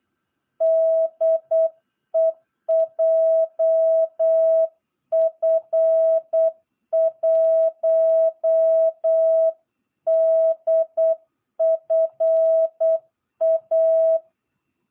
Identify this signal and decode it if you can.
Morse code, heard as a keyed tone, about 8 wpm, 660 Hz. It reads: DE JF1DFA
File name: jh8cbh / 2022cw